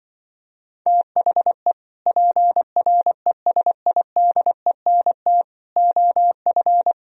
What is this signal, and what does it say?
Morse code, heard as a keyed tone, about 24 wpm, 700 Hz. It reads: THE PRESIDENT OF
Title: Speed-Racing - Sets of 4 Words 20wpm